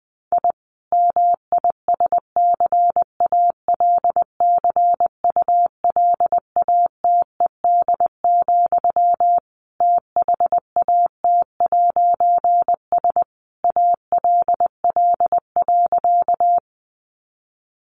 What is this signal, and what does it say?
Morse code, heard as a keyed tone, about 20 wpm, 700 Hz. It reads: I MISCALCULATED, THAT'S ALL.
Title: A Sound Of Thunder 20wpm